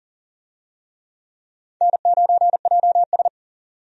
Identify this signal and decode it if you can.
Morse code, heard as a keyed tone, about 40 wpm, 700 Hz. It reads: N9JS
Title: USA Call Signs - 4 Characters - 40wpm